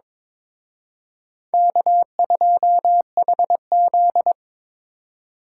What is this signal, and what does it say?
Morse code, heard as a keyed tone, about 26 wpm, 700 Hz. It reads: K2HZ
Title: Speed Racing - USA 1x2 Call Signs 22wpm